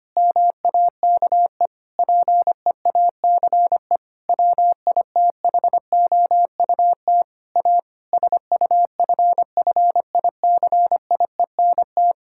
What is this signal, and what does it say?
Morse code, heard as a keyed tone, about 25 wpm, 700 Hz. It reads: MAKE PEACE WITHOUT A SUFFICIENT
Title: Sentences from Top 1600 Words 25wpm